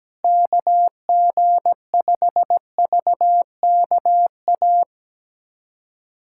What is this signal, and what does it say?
Morse code, heard as a keyed tone, about 17 wpm, 700 Hz. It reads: KG5VKA
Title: USA Callsign DE Callsign - 6 Characters - 17wpm